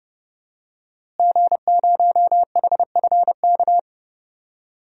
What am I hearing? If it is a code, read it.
Morse code, heard as a keyed tone, about 30 wpm, 700 Hz. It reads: G0HFK
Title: DXCC-100 Call Signs - 5 Characters - Repeated 3x - 30wpm